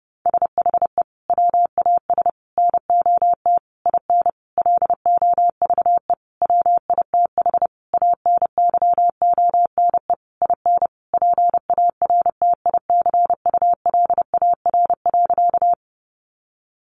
Morse code, heard as a keyed tone, about 30 wpm, 700 Hz. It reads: SHE WAS NOT IN LOVE WITH ANYONE IN PARTICULAR.
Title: Sentences from Top 900 Words 30wpm